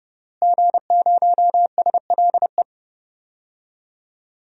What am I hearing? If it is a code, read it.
Morse code, heard as a keyed tone, about 30 wpm, 700 Hz. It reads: G0SLE
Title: DXCC-100 Call Signs - 5 Characters - Repeated 2x - 30wpm